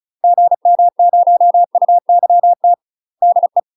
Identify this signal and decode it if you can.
Morse code, heard as a keyed tone, about 34 wpm, 700 Hz. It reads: GM0UYT DE
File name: UK MASTER QSOs TRAINING / 35WPM / UK MASTER QSOs 0040